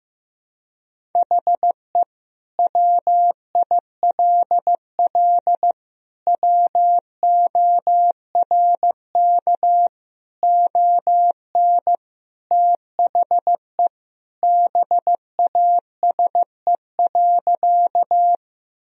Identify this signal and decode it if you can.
Morse code, heard as a keyed tone, about 15 wpm, 700 Hz. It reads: HE WILL WORK ON THE BASE.
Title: Sentences from Top 1000 Words - Review Part 1 - 15wpm